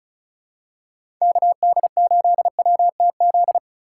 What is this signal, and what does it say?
Morse code, heard as a keyed tone, about 34 wpm, 700 Hz. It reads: KD8WTZ
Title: USA Call Signs - 2x Character Spacing - 35wpm